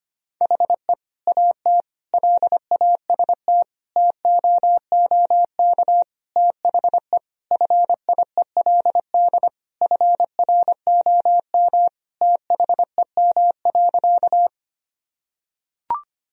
Morse code, heard as a keyed tone, about 25 wpm, 700 Hz. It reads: HE AT LAST TOOK THE FIELD FROM THEM.
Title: Sentences from Top 1000 Words - Review Part 2 - 25wpm